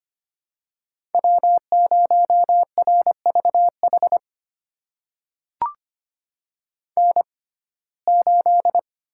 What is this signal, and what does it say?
Morse code, heard as a keyed tone, about 25 wpm, spaced as for 18 wpm, 700 Hz. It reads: W0RVH N8
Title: USA Call Signs - 5 Characters - 6x Character Spacing - 25wpm